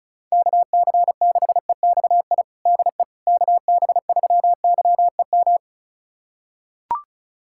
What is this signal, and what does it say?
Morse code, heard as a keyed tone, about 36 wpm, 700 Hz. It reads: KC6EXI DE KB3YEM
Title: USA Callsign DE Callsign 35wpm